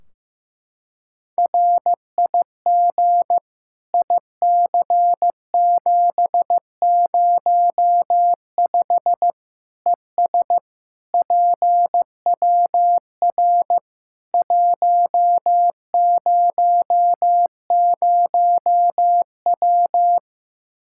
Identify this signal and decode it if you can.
Morse code, heard as a keyed tone, about 15 wpm, 700 Hz. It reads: RIG IC705 ES PWR 100W
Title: Practice QSO - Exchange 2 - Info - 15wpm